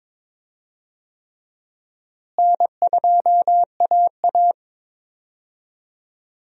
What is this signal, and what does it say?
Morse code, heard as a keyed tone, about 22 wpm, 700 Hz. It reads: N2AA
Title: USA Call Signs - 4 Characters - Repeated 3x - 22wpm